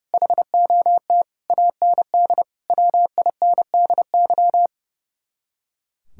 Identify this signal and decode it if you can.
Morse code, heard as a keyed tone, about 37 wpm, 700 Hz. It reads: HOT AND WINDY
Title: Speed-Racing Common QSO Phrases Encore - 30wpm